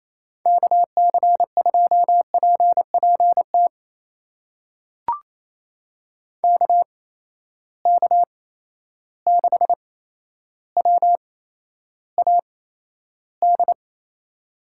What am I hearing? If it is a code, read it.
Morse code, heard as a keyed tone, about 27 wpm, spaced as for 12 wpm, 700 Hz. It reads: KC2PPT KK6WAD
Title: USA Call Signs - 6 Characters - 8x Character Spacing - 28wpm